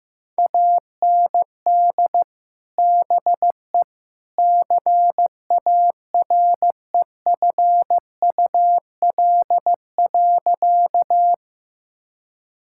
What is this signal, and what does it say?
Morse code, heard as a keyed tone, about 15 wpm, 700 Hz. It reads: AND BE CAREFUL.
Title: Sentences from Top 1600 Words 15wpm